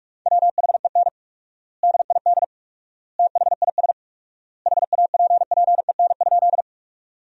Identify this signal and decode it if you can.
Morse code, heard as a keyed tone, about 46 wpm, 700 Hz. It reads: WHEN DID THIS HAPPEN?
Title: 4x Word-Spacing - Sentences from Top 400 Words - 45wpm